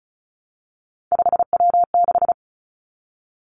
Morse code, heard as a keyed tone, about 36 wpm, 700 Hz. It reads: 5W6
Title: Three Challenging Letter-Number Combinations 35wpm